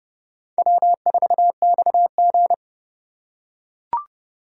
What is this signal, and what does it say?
Morse code, heard as a keyed tone, about 30 wpm, 700 Hz. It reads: W4XG